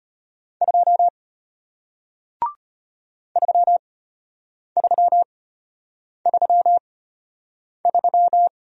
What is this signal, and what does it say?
Morse code, heard as a keyed tone, about 34 wpm, spaced as for 8 wpm, 700 Hz. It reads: 2 3333